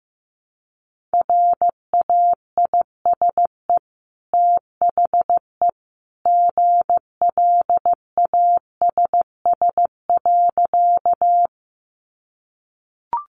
Morse code, heard as a keyed tone, about 15 wpm, 700 Hz. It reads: RAISE THE GLASS.
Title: Sentences from Top 1400 Words 15wpm